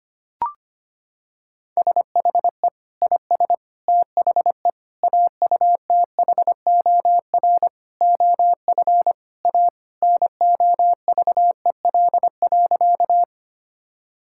Morse code, heard as a keyed tone, about 25 wpm, 700 Hz. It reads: SHE IS THE AUTHOR OF A NOVEL.
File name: practice / 25wpm / Sentences with 4-Word Phrases - 25wpm